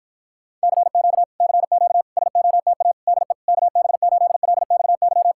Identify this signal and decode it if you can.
Morse code, heard as a keyed tone, about 48 wpm, 700 Hz. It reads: CQ CQ SOTA DE KB8LXY